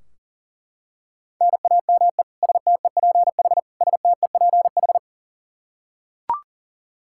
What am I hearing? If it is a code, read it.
Morse code, heard as a keyed tone, about 40 wpm, 700 Hz. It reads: NAME STEPH STEPH
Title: Practice QSO - Exchange 1 - Beginning - 40wpm